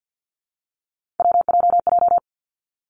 Morse code, heard as a keyed tone, about 50 wpm, 700 Hz. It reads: WJJ
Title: Three Challenging Letter-Number Combinations 50wpm